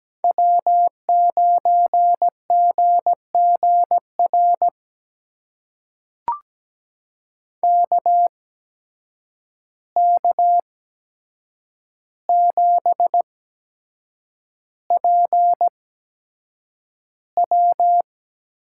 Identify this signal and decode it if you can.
Morse code, heard as a keyed tone, about 17 wpm, spaced as for 7 wpm, 700 Hz. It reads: W9GGR KK7PW
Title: USA Call Signs - 5 Characters - 8x Character Spacing - 17wpm